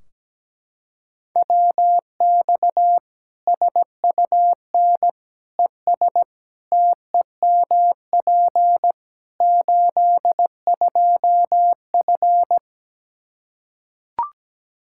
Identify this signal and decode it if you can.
Morse code, heard as a keyed tone, about 17 wpm, 700 Hz. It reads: WX SUN ES TEMP 82F
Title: K1USN SST QSO Practice 17wpm